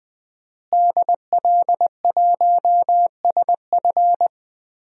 Morse code, heard as a keyed tone, about 20 wpm, 700 Hz. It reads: DL1SF
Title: DXCC-100 Call Signs - 5 Characters - Repeated 2x - 20wpm